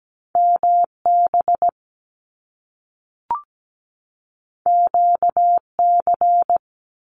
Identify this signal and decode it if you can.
Morse code, heard as a keyed tone, about 17 wpm, 700 Hz. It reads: MB QC
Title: Canadian Provinces and Territories 17wpm